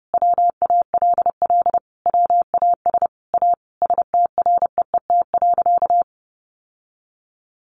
Morse code, heard as a keyed tone, about 28 wpm, 700 Hz. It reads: WALL WAS A STREET.
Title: Sentences from Top 1100 Words 30wpm